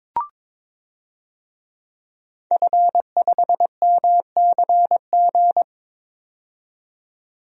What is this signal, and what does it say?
Morse code, heard as a keyed tone, about 22 wpm, 700 Hz. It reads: F5MCG
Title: DXCC-100 Call Signs - 5 Characters - 22wpm